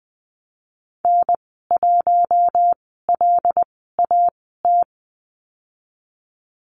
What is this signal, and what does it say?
Morse code, heard as a keyed tone, about 20 wpm, spaced as for 14 wpm, 700 Hz. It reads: N1LAT
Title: 082 Lesson 14g - Number 2 - Call Signs 20wpm